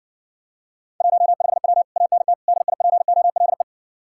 Morse code, heard as a keyed tone, about 60 wpm, 700 Hz. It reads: 15W ANT DIPOLE